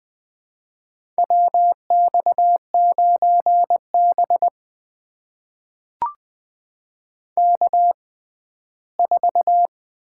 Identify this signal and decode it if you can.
Morse code, heard as a keyed tone, about 20 wpm, spaced as for 14 wpm, 700 Hz. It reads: WX9B K4
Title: USA Call Signs - 4 Characters - 6x Character Spacing - 20wpm